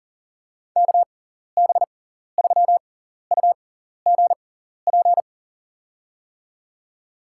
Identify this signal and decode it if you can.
Morse code, heard as a keyed tone, about 40 wpm, spaced as for 14 wpm, 700 Hz. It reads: KB3UGP
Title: USA Call Signs - 6 Characters - 6x Character Spacing - 40wpm